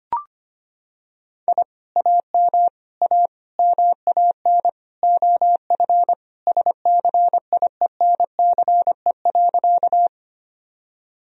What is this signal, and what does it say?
Morse code, heard as a keyed tone, about 25 wpm, 700 Hz. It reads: I AM A MAN OF SCIENCE.
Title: Sentences from Top 1000 Words - Review Part 2 - 25wpm